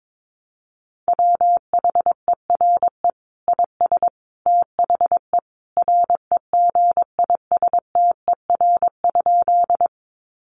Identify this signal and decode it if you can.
Morse code, heard as a keyed tone, about 22 wpm, 700 Hz. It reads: WHERE IS THE REGISTER?
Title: Sentences from Top 1200 Words 22wpm